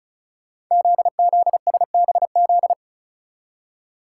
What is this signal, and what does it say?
Morse code, heard as a keyed tone, about 36 wpm, 700 Hz. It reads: ZZSBZ
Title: Code Groups - 5 Letters - 35wpm